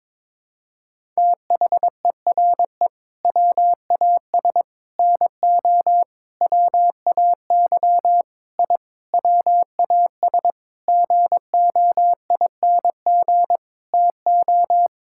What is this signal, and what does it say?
Morse code, heard as a keyed tone, about 22 wpm, 700 Hz. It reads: THERE WAS NO WAY I WAS GOING TO
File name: practice / 22wpm / Sets of 8 Words - 22wpm